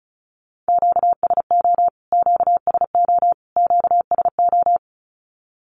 Morse code, heard as a keyed tone, about 34 wpm, 700 Hz. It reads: QSO QSO QSO
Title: Q-Codes Repeated 3x 35wpm